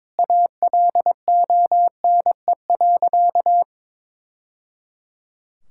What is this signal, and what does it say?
Morse code, heard as a keyed tone, about 22 wpm, 700 Hz. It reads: ALONE.
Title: Sentences from Top 1000 Words - Review Part 6 - 22wpm